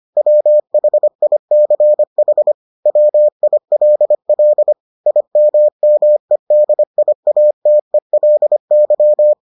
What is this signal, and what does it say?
Morse code, heard as a keyed tone, about 25 wpm, 600 Hz. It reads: WHICH WILL IMMEDIATELY